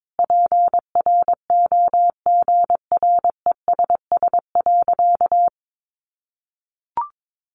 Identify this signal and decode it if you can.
Morse code, heard as a keyed tone, about 22 wpm, 700 Hz. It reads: PROGRESS.
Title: Sentences from Top 1400 Words 22wpm